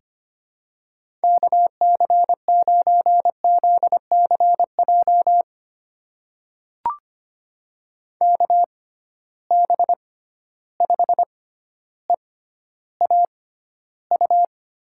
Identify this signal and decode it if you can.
Morse code, heard as a keyed tone, about 25 wpm, spaced as for 13 wpm, 700 Hz. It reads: KC9ZCJ KB5EAU
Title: USA Call Signs - 6 Characters - 6x Character Spacing - 25wpm